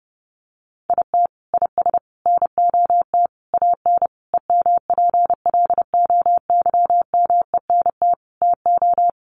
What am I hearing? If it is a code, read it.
Morse code, heard as a keyed tone, about 28 wpm, 700 Hz. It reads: IT IS NOT AN EMPLOYMENT TO
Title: Sentences from Top 1500 Words 30wpm